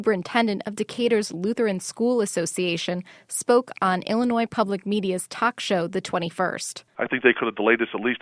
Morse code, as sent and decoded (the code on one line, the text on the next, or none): none